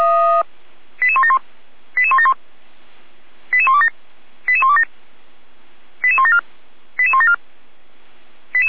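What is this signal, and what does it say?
Five-tone selective call (ZVEI), heard as quick runs of five tones, tones of about 70 ms. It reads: 88171 88171 88127 88127 88174 88174
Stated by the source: ZVEI five-tone selective call received with an SDR.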